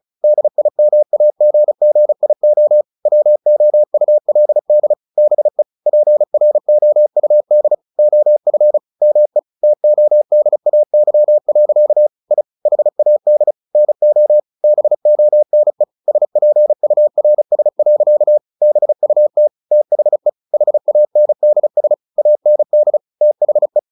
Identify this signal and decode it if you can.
Morse code, heard as a keyed tone, about 35 wpm, 600 Hz. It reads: DIMAGGIO WOULD BE PROUD OF ME TODAY. I HAD NO BONE SPURS. BUT THE HANDS AND THE
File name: the old man and the sea 2 35wpm